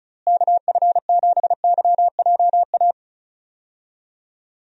Morse code, heard as a keyed tone, about 34 wpm, 700 Hz. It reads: KF7YJA
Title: USA Call Signs - 6 Characters - Repeated 3x - 35wpm